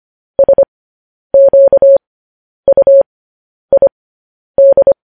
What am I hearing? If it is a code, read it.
Morse code, heard as a keyed tone, about 25 wpm, spaced as for 10 wpm, 550 Hz. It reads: SQUID
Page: News Headlines in Morse Code at 10 WPM